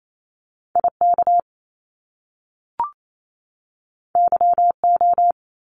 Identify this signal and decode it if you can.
Morse code, heard as a keyed tone, about 28 wpm, 700 Hz. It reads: IK YO
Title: Top Two Letter Combinations in Words 28wpm